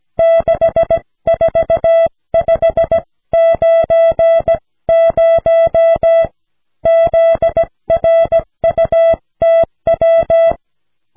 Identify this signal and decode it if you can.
Morse code, heard as a keyed tone, about 17 wpm, 660 Hz. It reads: 64590 ZRUTW
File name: cw / snelheid 4